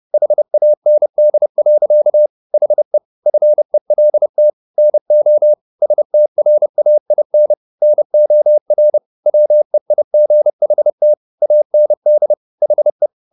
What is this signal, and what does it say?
Morse code, heard as a keyed tone, about 30 wpm, 600 Hz. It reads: HAND. HE FELT NO STRAIN NOR WEIGHT AND HE
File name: the old man and the sea / the old man and the sea 1 30wpm